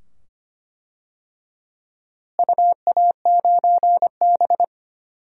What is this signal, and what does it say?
Morse code, heard as a keyed tone, about 25 wpm, 700 Hz. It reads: UA9B
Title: DXCC-100 Call Signs - 4 Characters - Repeated 2x - 25wpm